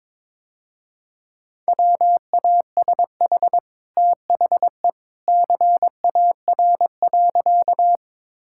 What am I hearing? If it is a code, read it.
Morse code, heard as a keyed tone, about 22 wpm, 700 Hz. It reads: WASH THE CAR.
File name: practice / 22wpm / Easy Sentences with Hints - 22wpm